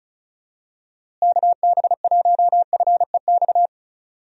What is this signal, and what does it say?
Morse code, heard as a keyed tone, about 36 wpm, 700 Hz. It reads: KB1FEX
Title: USA Call Signs - 6 Characters - Repeated 2x - 35wpm